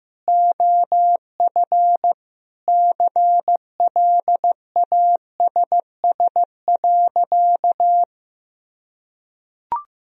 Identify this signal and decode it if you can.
Morse code, heard as a keyed tone, about 16 wpm, 700 Hz. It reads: OF CLASS.
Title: Sentences from Top 1000 Words - Review Part 2 - 15wpm